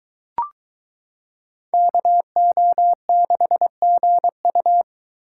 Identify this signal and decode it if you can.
Morse code, heard as a keyed tone, about 23 wpm, 700 Hz. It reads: KO6GU